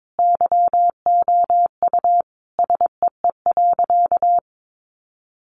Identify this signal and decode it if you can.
Morse code, heard as a keyed tone, about 22 wpm, 700 Hz. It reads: YOU SEE.
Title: Sentences from Top 600 Words 22wpm